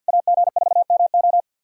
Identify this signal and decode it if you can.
Morse code, heard as a keyed tone, about 50 wpm, 700 Hz. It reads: AG4MO
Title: SOTA QSO Practice 50wpm